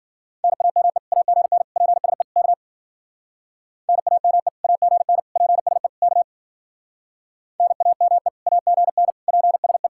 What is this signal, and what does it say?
Morse code, heard as a keyed tone, about 51 wpm, 700 Hz. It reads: NAME AGN PSE K NAME AGN PSE K NAME AGN PSE